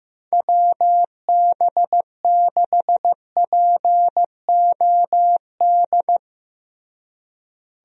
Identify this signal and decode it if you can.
Morse code, heard as a keyed tone, about 16 wpm, 700 Hz. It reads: WB6POD
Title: USA Call Signs - 6 Characters - Repeated 3x - 15wpm